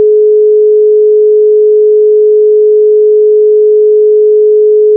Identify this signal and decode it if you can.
Dial tone (Europe, 425 Hz).
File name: commit / freizeichen